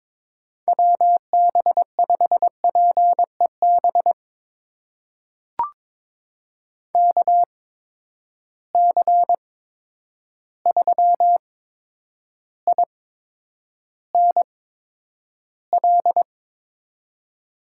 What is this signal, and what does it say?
Morse code, heard as a keyed tone, about 22 wpm, spaced as for 9 wpm, 700 Hz. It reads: WB5PEB KC3INL
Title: USA Call Signs - 6 Characters - 8x Character Spacing - 22wpm